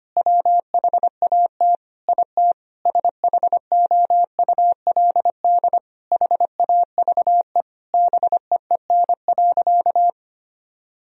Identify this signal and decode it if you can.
Morse code, heard as a keyed tone, about 25 wpm, 700 Hz. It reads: WHAT IT SHOULD HAVE BEEN.
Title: Sentences from Top 1000 Words - Review Part 4 - 25wpm